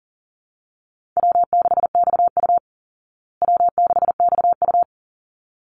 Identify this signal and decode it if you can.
Morse code, heard as a keyed tone, about 40 wpm, 700 Hz. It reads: W6XU W6XU
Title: Easy Call Signs Repeated 2x - 40wpm